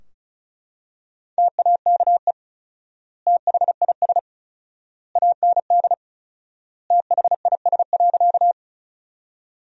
Morse code, heard as a keyed tone, about 36 wpm, 700 Hz. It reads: TAKE THIS AND THIS.
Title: 4x Word-Spacing - Sentences from Top 200 Words - 35wpm